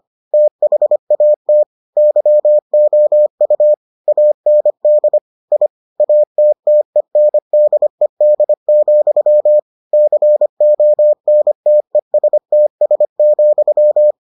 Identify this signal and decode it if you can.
Morse code, heard as a keyed tone, about 25 wpm, 600 Hz. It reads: THAT YOU AND I ATTENDED, CONTESTS,